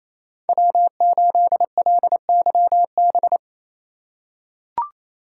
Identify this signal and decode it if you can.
Morse code, heard as a keyed tone, about 29 wpm, 700 Hz. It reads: W8LYB
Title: USA Call Signs - 5 Characters - 2x Character Spacing - 28wpm